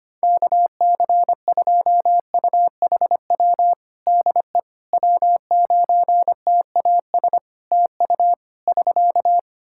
Morse code, heard as a keyed tone, about 25 wpm, 700 Hz. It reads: KC2UHW DE W9TAS TU <SK>
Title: Practice QSO - Exchange 4 - Ending - 25wpm